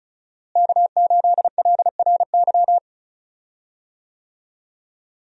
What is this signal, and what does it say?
Morse code, heard as a keyed tone, about 36 wpm, 700 Hz. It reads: K8LRY
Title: USA Call Signs - 5 Characters - Repeated 3x - 35wpm